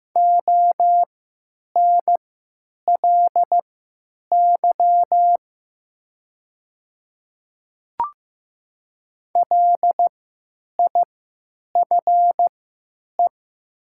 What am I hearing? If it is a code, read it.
Morse code, heard as a keyed tone, about 15 wpm, spaced as for 9 wpm, 700 Hz. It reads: ONLY LIFE
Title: 3x Character Spacing - Top 300 Words - 15wpm